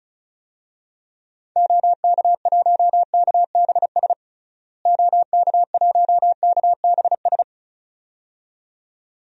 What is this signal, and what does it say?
Morse code, heard as a keyed tone, about 36 wpm, 700 Hz. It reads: OK1KBS OK1KBS
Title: DXCC-100 Call Signs - 6 Characters - Repeated 2x - 35wpm